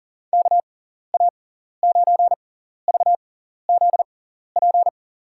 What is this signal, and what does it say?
Morse code, heard as a keyed tone, about 38 wpm, spaced as for 14 wpm, 700 Hz. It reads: KA9VZP
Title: USA Call Signs - 6 Characters - 6x Character Spacing - 40wpm